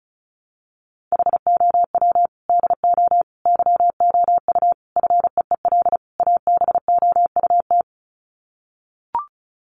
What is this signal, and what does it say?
Morse code, heard as a keyed tone, about 34 wpm, 700 Hz. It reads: HOW DO YOU FEEL ABOUT
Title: Sets of 5 Words 35wpm